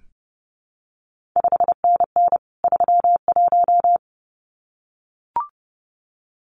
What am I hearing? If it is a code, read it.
Morse code, heard as a keyed tone, about 30 wpm, 700 Hz. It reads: 5NN 31
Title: CQ WW DX Contest Exchange - 30wpm